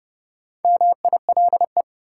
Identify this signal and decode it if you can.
Morse code, heard as a keyed tone, about 30 wpm, 700 Hz. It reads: MILE
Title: Sentences from Top 400 Words - New Word Drill - 30wpm